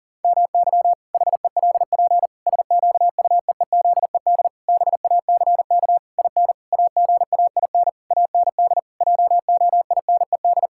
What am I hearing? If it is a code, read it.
Morse code, heard as a keyed tone, about 40 wpm, 700 Hz. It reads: MY HELP SQUEEZED BACK IN AGAIN AND JOINED
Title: Pangram Challenge v2 - 40wpm